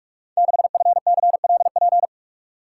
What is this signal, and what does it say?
Morse code, heard as a keyed tone, about 45 wpm, 700 Hz. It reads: 6FCLP
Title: Code Groups - 5 Letter:Numbers - 45wpm